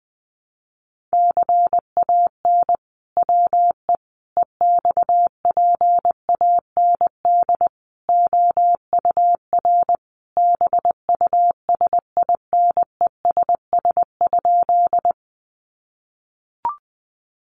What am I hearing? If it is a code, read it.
Morse code, heard as a keyed tone, about 20 wpm, 700 Hz. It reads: CAN WE EXPAND OUR BUSINESS?
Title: Sentences from Top 1400 Words 20wpm